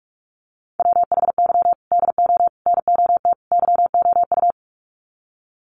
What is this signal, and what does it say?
Morse code, heard as a keyed tone, about 44 wpm, 700 Hz. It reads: WHY DO NOT YOU
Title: Sets of 4 Words 45wpm